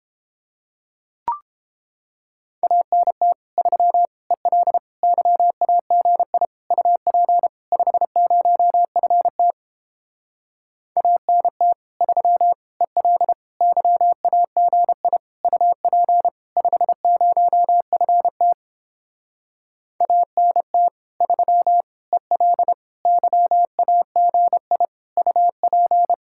Morse code, heard as a keyed tone, about 29 wpm, 700 Hz. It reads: ANT 3 EL YAGI UP 50FT ANT 3 EL YAGI UP 50FT ANT 3 EL YAGI UP